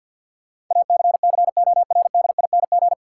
Morse code, heard as a keyed tone, about 52 wpm, 700 Hz. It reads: ACCORDING
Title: Sentences from Top 700 Words - New Word Drill - 50wpm